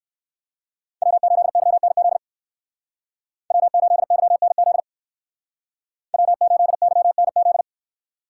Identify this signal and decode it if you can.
Morse code, heard as a keyed tone, about 64 wpm, 700 Hz. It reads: W8YNZ W8YNZ W8YNZ